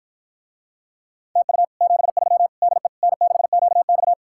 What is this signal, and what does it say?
Morse code, heard as a keyed tone, about 54 wpm, 700 Hz. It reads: TU 73 DE N6QX